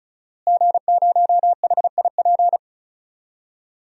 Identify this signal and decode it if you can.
Morse code, heard as a keyed tone, about 36 wpm, 700 Hz. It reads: G0HIP
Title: DXCC-100 Call Signs - 5 Characters - Repeated 3x - 35wpm